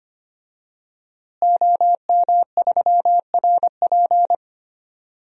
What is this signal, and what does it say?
Morse code, heard as a keyed tone, about 25 wpm, 700 Hz. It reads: OM3RP
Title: DXCC-100 Call Signs - 5 Characters - Repeated 3x - 25wpm